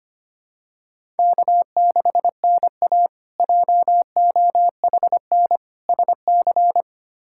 Morse code, heard as a keyed tone, about 25 wpm, 700 Hz. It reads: K6NA JOHN SC
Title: K1USN SST QSO Practice - 25wpm